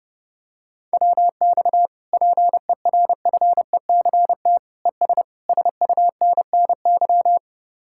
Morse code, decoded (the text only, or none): WX PERFECT ES SUNNY